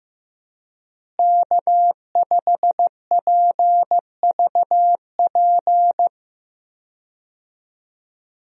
Morse code, heard as a keyed tone, about 15 wpm, 700 Hz. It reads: K5PVP
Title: USA Call Signs - 5 Characters - Repeated 3x - 15wpm